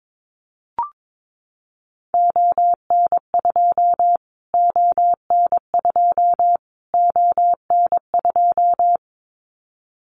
Morse code, heard as a keyed tone, about 22 wpm, 700 Hz. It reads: ON2 ON2 ON2
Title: Callsign Prefixes for Top 25 DXCC - Repeated 3x - 22wpm